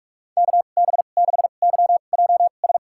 Morse code, heard as a keyed tone, about 46 wpm, spaced as for 33 wpm, 700 Hz. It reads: KB6YJS
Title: USA Call Signs - 2x Character Spacing - 45wpm